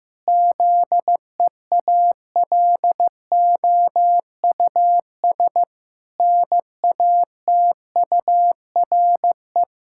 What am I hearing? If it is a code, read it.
Morse code, heard as a keyed tone, about 15 wpm, 700 Hz. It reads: ZEALOUS NATURE
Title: Pangram Challenge v2 - 15wpm